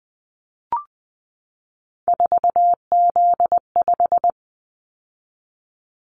Morse code, heard as a keyed tone, about 20 wpm, 700 Hz. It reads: 4Z5
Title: Callsign Prefixes for Top 50 DXCC - 20wpm